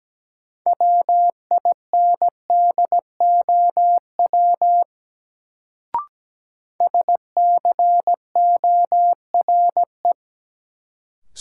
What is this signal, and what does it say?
Morse code, heard as a keyed tone, about 17 wpm, 700 Hz. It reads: WINDOW SCORE
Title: Sentences from Top 1600 Words - New Word Drill - 17wpm